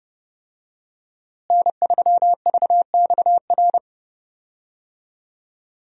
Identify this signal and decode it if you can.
Morse code, heard as a keyed tone, about 30 wpm, 700 Hz. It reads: N3VXR
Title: USA Call Signs - 5 Characters - Repeated 3x - 30wpm